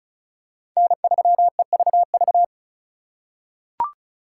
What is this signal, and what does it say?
Morse code, heard as a keyed tone, about 34 wpm, 700 Hz. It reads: N3EVV
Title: USA Call Signs - 5 Characters - 4x Character Spacing - 35wpm